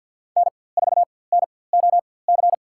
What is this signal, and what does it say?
Morse code, heard as a keyed tone, about 51 wpm, spaced as for 23 wpm, 690 Hz. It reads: N4NOC
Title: USA Call Signs - 5 Characters - 4x Character Spacing - 50wpm